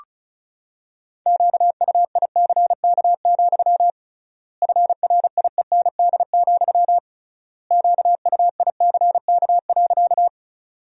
Morse code, heard as a keyed tone, about 36 wpm, 700 Hz. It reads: QUICK, FRIEND, QUICK.
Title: 3x Word-Spacing - Sentences from Top 500 Words - 35wpm